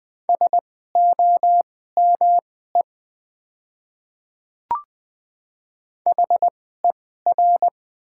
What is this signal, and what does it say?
Morse code, heard as a keyed tone, about 20 wpm, spaced as for 15 wpm, 700 Hz. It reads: SOME HER
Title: 2x Character Spacing - Top 100 Words - 20wpm